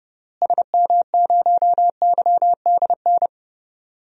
Morse code, heard as a keyed tone, about 30 wpm, 700 Hz. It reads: SM0YDN
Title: DXCC-100 Call Signs - 6 Characters - Repeated 2x - 30wpm